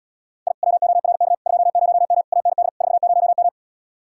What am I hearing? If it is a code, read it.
Morse code, heard as a keyed tone, about 80 wpm, 700 Hz. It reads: I WORK 20M AND 40M